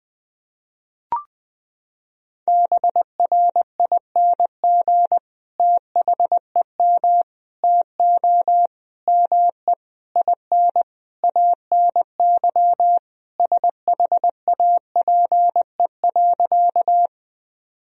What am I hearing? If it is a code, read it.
Morse code, heard as a keyed tone, about 21 wpm, 700 Hz. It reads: BRING THEM TO ME IN ANY SHAPE.
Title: Sentences from Top 1000 Words - Review Part 3 - 20wpm